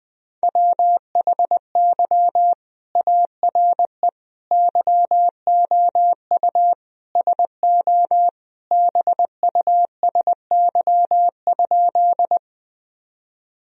Morse code, heard as a keyed tone, about 20 wpm, 700 Hz. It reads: WHY ARE YOU SO BUSY?
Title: Sentences from Top 1000 Words - Review Part 3 - 20wpm